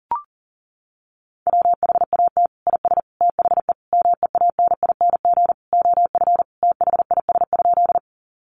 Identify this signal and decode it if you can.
Morse code, heard as a keyed tone, about 40 wpm, 700 Hz. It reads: WHAT IS THE MEANING OF THIS?
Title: Sentences from Top 900 Words 40wpm